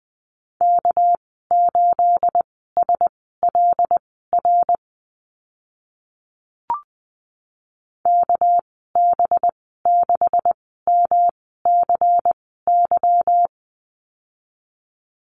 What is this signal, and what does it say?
Morse code, heard as a keyed tone, about 20 wpm, spaced as for 14 wpm, 700 Hz. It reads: K8SLR KB6MCY
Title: 213 Lesson 34g - Number 8 - Call Signs 20wpm